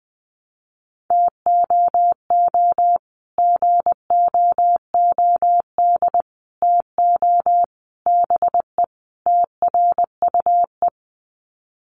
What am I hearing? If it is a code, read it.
Morse code, heard as a keyed tone, about 19 wpm, 700 Hz. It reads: TOO GOOD TO BE TRUE
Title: Sets of 5 Words Encore 20wpm